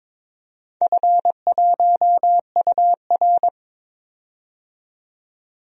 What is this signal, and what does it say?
Morse code, heard as a keyed tone, about 23 wpm, 700 Hz. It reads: F1UR
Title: DXCC-100 Call Signs - 4 Characters - 22wpm